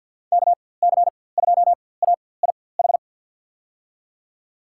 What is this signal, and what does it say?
Morse code, heard as a keyed tone, about 51 wpm, spaced as for 23 wpm, 700 Hz. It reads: KC2AIH
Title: USA Call Signs - 4x Character Spacing - 50wpm